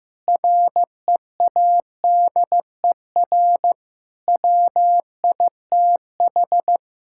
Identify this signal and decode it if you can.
Morse code, heard as a keyed tone, about 14 wpm, 700 Hz. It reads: READER WITH
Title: Pangram Challenge v2 - 15wpm